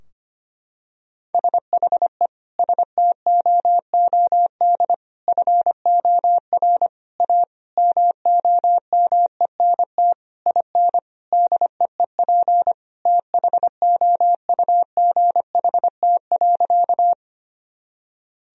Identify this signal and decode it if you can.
Morse code, heard as a keyed tone, about 25 wpm, 700 Hz. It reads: SHE STOOD FOR A MOMENT IN DEEP THOUGHT.
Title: Sentences from Top 1000 Words - Review Part 3 - 25wpm